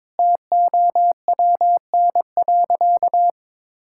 Morse code, heard as a keyed tone, about 22 wpm, 700 Hz. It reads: TOWN.
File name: practice / 22wpm / Sentences with 3-Word Phrases - 22wpm